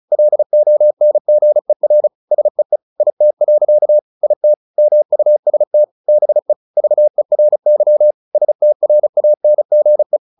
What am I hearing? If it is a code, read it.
Morse code, heard as a keyed tone, about 34 wpm, 600 Hz. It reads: LONGER SEE IT. IT MUST BE VERY STRANGE
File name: the old man and the sea 2 35wpm